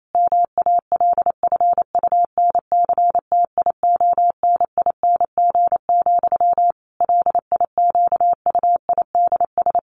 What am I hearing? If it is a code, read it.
Morse code, heard as a keyed tone, about 28 wpm, 700 Hz. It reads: MALFUNCTIONING, LIQUIDS
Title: A Sound Of Thunder 28wpm